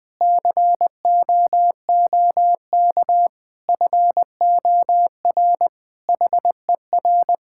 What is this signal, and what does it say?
Morse code, heard as a keyed tone, about 20 wpm, 700 Hz. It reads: COOK FOR HER
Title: Sentences from Top 1000 Words - Review Part 5 - 20wpm